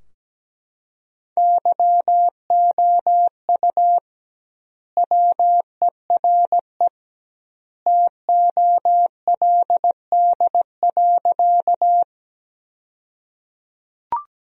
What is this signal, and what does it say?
Morse code, heard as a keyed tone, about 17 wpm, 700 Hz. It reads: YOU WERE TOLD.
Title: 2x Word-Spacing - Sentences from Top 400 Words - 17wpm